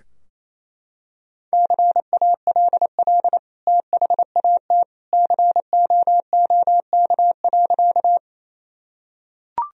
Morse code, heard as a keyed tone, about 28 wpm, 700 Hz. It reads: CALL THAT COOK.
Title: Sentences from Top 1000 Words - Review Part 4 - 28wpm